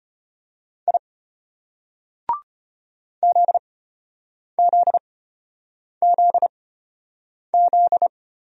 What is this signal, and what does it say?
Morse code, heard as a keyed tone, about 33 wpm, spaced as for 8 wpm, 700 Hz. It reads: I ZZZZ